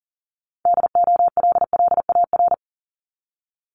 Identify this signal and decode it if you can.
Morse code, heard as a keyed tone, about 40 wpm, 700 Hz. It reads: DOLLAR
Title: Multi-letter Phonemes Repeated 3x 40wpm